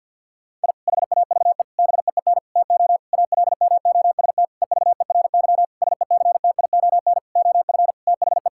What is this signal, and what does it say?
Morse code, heard as a keyed tone, about 50 wpm, 700 Hz. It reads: I HAVE BEEN TO ALMOST EVERY SECTION OF THE
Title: Sentences from Top 1000 Words - Review Part 5 - 50wpm